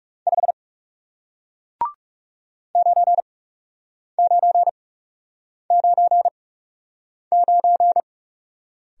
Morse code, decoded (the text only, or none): 5 9999